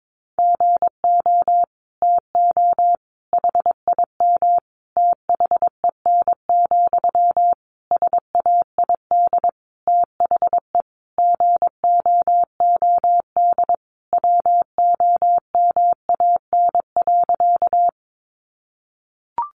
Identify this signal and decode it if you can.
Morse code, heard as a keyed tone, about 22 wpm, 700 Hz. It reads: GO TO HIM THEN, SAID THE GOOD WOMAN.
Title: Sentences from Top 1100 Words 22wpm